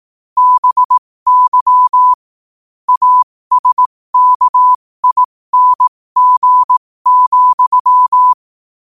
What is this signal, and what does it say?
Morse code, heard as a keyed tone, about 18 wpm, spaced as for 15 wpm, 1000 Hz. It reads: BY ASKING,
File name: qotdfiles / 15wpm